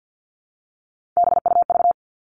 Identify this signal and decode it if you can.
Morse code, heard as a keyed tone, about 48 wpm, 700 Hz. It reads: BUV
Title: All Cut Numbers - 3-Digit 50wpm